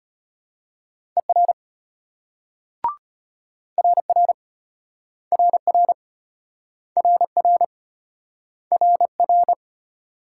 Morse code, heard as a keyed tone, about 33 wpm, 700 Hz. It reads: ER RR RR RR RR